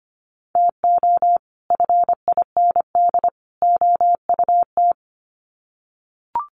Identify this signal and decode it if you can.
Morse code, heard as a keyed tone, about 25 wpm, 700 Hz. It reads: TO FIND OUT
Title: Sets of 3 Words 25wpm